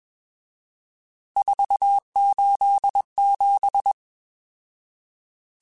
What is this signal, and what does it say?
Morse code, heard as a keyed tone, about 21 wpm, 800 Hz. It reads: 487